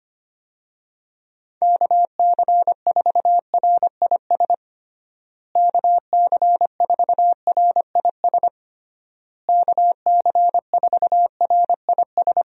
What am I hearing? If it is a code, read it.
Morse code, heard as a keyed tone, about 25 wpm, 700 Hz. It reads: KC4RIS KC4RIS KC4RIS
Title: USA Call Signs - 6 Characters - Repeated 3x - 25wpm